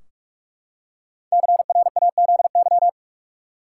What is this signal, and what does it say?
Morse code, heard as a keyed tone, about 43 wpm, 700 Hz. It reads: CRAZY
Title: Five Letter Words - 45wpm